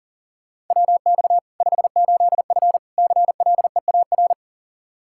Morse code, heard as a keyed tone, about 40 wpm, 700 Hz. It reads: WX 58F CLEAR
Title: Practice QSO - Exchange 2 - Info - 40wpm